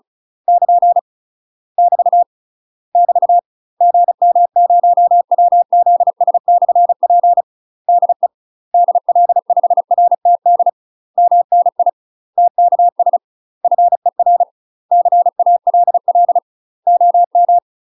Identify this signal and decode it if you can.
Morse code, heard as a keyed tone, about 34 wpm, 700 Hz. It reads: <KN> <BT> <BT> GM0WZS/P DE DL5RTD MNI TKS FER CALL OM
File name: UK MASTER QSOs TRAINING / 35WPM / UK MASTER QSOs 0004